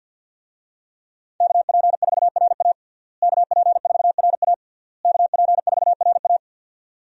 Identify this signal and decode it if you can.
Morse code, heard as a keyed tone, about 52 wpm, 700 Hz. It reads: KP4RA KP4RA KP4RA
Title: DXCC-100 Call Signs - 5 Characters - Repeated 3x - 50wpm